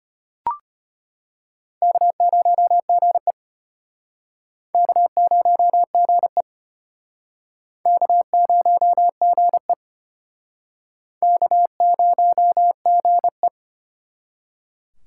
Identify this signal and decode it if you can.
Morse code, heard as a keyed tone, about 34 wpm, 700 Hz. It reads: K0GE K0GE K0GE K0GE